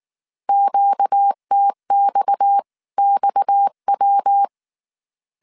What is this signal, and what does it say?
Morse code, heard as a keyed tone, about 19 wpm, 800 Hz. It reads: QTXXW